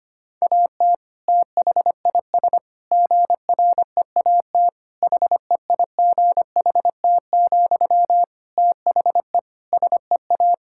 Morse code, heard as a keyed tone, about 25 wpm, 700 Hz. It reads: AT THIS GREAT HEIGHT, THE SEA
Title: Sentences from Top 1600 Words 25wpm